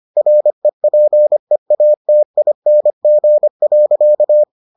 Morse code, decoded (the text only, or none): REPEATING.